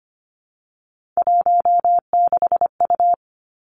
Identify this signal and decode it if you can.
Morse code, heard as a keyed tone, about 25 wpm, 700 Hz. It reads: 16U
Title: Three Challenging Letter-Number Combinations 25wpm